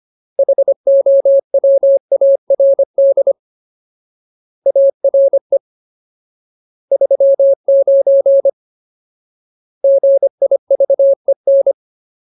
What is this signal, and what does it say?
Morse code, heard as a keyed tone, about 25 wpm, 550 Hz. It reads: HOWARD ARE 39 GIVEN
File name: Ultimate Word List / 25WPM Ultimate